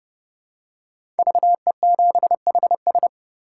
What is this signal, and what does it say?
Morse code, heard as a keyed tone, about 30 wpm, 700 Hz. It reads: VE7HS
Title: K1USN SST QSO Practice - 30wpm